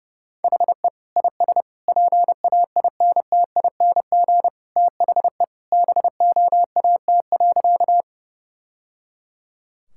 Morse code, heard as a keyed tone, about 30 wpm, 700 Hz. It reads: HE IS PAINTING THE BOAT.
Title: Sentences from Top 1000 Words - Review Part 6 - 30wpm